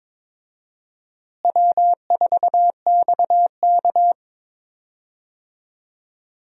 Morse code, heard as a keyed tone, about 22 wpm, 700 Hz. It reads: W4XK
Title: USA Call Signs - 4 Characters - Repeated 3x - 22wpm